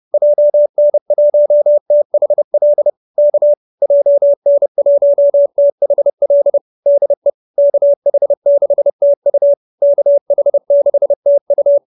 Morse code, heard as a keyed tone, about 30 wpm, 590 Hz. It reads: JN1THL K JN1THL DE KH6TU KH6TU
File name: QSO309 30